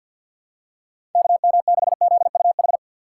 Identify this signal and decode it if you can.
Morse code, heard as a keyed tone, about 50 wpm, 700 Hz. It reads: KM6ZUH
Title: USA Call Signs - 6 Characters - 6x Character Spacing - 50wpm